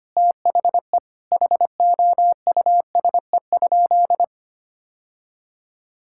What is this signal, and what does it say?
Morse code, heard as a keyed tone, about 25 wpm, 700 Hz. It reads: THE HOUSE?
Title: Sentences from Top 1000 Words - Review Part 5 - 25wpm